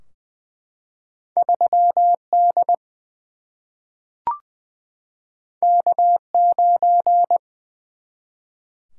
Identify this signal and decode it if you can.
Morse code, heard as a keyed tone, about 20 wpm, 700 Hz. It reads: 3 D K 9
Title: Single-Letter Single-Number 20wpm